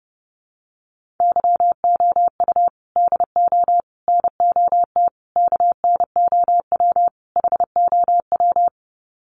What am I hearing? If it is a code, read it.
Morse code, heard as a keyed tone, about 30 wpm, 700 Hz. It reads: YOU DO NOT KNOW HOW
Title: Sets of 5 Words Encore 30wpm